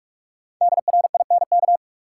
Morse code, heard as a keyed tone, about 46 wpm, 700 Hz. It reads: DRINK
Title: Sentences from Top 1500 Words - New Word Drill Alt - 45wpm